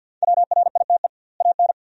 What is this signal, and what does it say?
Morse code, heard as a keyed tone, about 50 wpm, 700 Hz. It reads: WRITE AN
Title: 2x Word-Spacing - Sentences from Top 300 Words - 50wpm